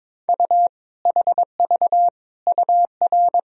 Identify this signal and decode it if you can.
Morse code, heard as a keyed tone, about 22 wpm, 700 Hz. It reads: U HV UR
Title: Sentences from Top 400 Words and Abbreviations 22wpm